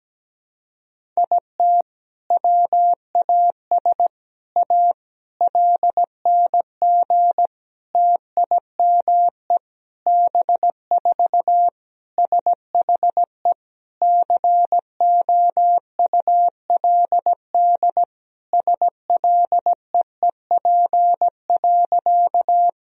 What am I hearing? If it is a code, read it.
Morse code, heard as a keyed tone, about 17 wpm, 700 Hz. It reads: IT WAS A LNG TIME B4 SHE COULD SLEEP.
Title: Sentences from Top 500 Words and Abbreviations 17wpm